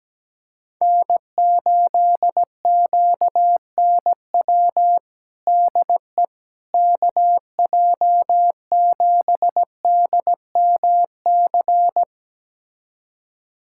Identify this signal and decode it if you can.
Morse code, heard as a keyed tone, about 17 wpm, 700 Hz. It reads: N8QNW DE KJ7DMC
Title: USA Callsign DE Callsign 17wpm